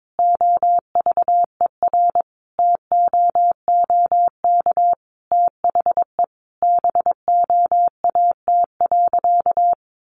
Morse code, heard as a keyed tone, about 22 wpm, 700 Hz. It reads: OVERTOOK THE BOAT.
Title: Sentences from Top 1400 Words 22wpm